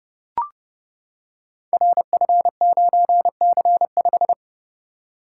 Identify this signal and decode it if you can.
Morse code, heard as a keyed tone, about 30 wpm, 700 Hz. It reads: RF9C5
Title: Code Groups - 5 Letter:Numbers - 30wpm